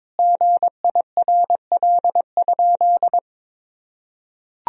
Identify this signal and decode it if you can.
Morse code, heard as a keyed tone, about 22 wpm, 700 Hz. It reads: GIRL?
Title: Sentences from Top 1000 Words - Review Part 1 - 22wpm